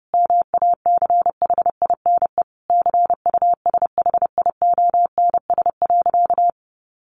Morse code, heard as a keyed tone, about 30 wpm, 700 Hz. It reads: MACHINE CUSHIONS.
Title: A Sound Of Thunder 30wpm